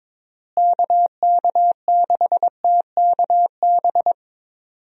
Morse code, heard as a keyed tone, about 22 wpm, 700 Hz. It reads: KK6TKB
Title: USA Call Signs - 6 Characters - Repeated 3x - 22wpm